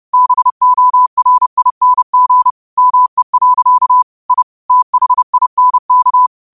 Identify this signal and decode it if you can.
Morse code, heard as a keyed tone, about 30 wpm, 1000 Hz. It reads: BORING ME. I THINK
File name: qotdfiles / 30wpm